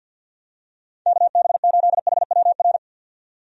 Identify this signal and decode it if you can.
Morse code, heard as a keyed tone, about 52 wpm, 700 Hz. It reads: KB8HWR
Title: USA Call Signs - 6 Characters - 50wpm